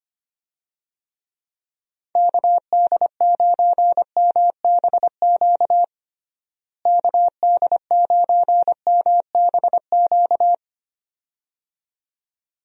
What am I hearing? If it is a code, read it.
Morse code, heard as a keyed tone, about 25 wpm, 700 Hz. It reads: KD9MBQ KD9MBQ
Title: USA Call Signs - 6 Characters - Repeated 2x - 25wpm